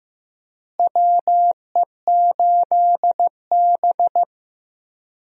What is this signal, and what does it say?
Morse code, heard as a keyed tone, about 15 wpm, 700 Hz. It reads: WE8B
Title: USA Call Signs - 4 Characters - Repeated 2x - 15wpm